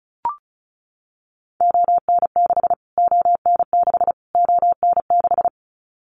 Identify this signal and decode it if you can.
Morse code, heard as a keyed tone, about 34 wpm, 700 Hz. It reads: ON6 ON6 ON6
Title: Callsign Prefixes for Top 25 DXCC - Repeated 3x - 35wpm